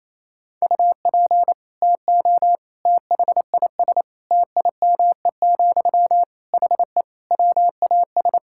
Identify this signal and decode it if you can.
Morse code, heard as a keyed tone, about 28 wpm, 700 Hz. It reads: UP TO THIS TIME, HE WAS
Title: Sentences from Top 1000 Words - Review Part 6 - 28wpm